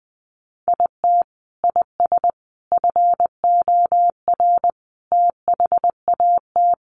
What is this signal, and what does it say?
Morse code, heard as a keyed tone, about 20 wpm, 700 Hz. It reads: IT IS FOR THAT
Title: Sentences from Top 700 Words 20wpm